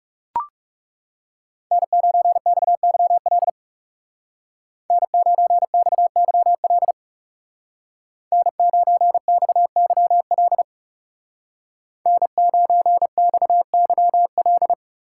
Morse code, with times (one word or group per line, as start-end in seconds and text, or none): 1.7-3.5 N9XYL
4.9-6.9 N9XYL
8.3-10.6 N9XYL
12.1-14.7 N9XYL